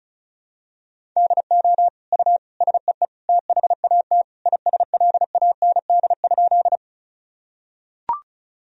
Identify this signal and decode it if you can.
Morse code, heard as a keyed tone, about 36 wpm, 700 Hz. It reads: DO U SEE THAT ISLAND?
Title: Sentences from Top 500 Words and Abbreviations 35wpm